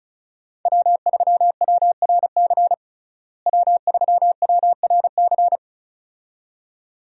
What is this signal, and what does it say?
Morse code, heard as a keyed tone, about 34 wpm, 700 Hz. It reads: W3WRC W3WRC
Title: USA Call Signs - 5 Characters - Repeated 2x - 35wpm